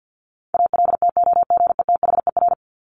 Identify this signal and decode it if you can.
Morse code, heard as a keyed tone, about 50 wpm, 700 Hz. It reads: ALTOGETHER
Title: Top 1501-1750 Words 50wpm